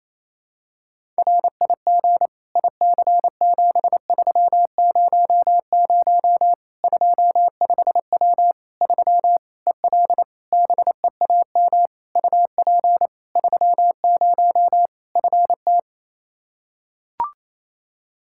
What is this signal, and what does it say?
Morse code, heard as a keyed tone, about 28 wpm, 700 Hz. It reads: RIG IC7300 25W 3 EL BEAM UP 30 FT
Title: Practice QSO - Exchange 2 - Info - 28wpm